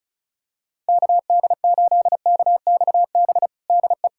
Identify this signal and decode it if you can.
Morse code, heard as a keyed tone, about 36 wpm, 700 Hz. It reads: KD8KXB DE
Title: USA Callsign DE Callsign 35wpm